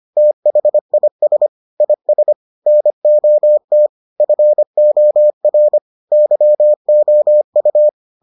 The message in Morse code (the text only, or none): THIS IS NOT FOR YOU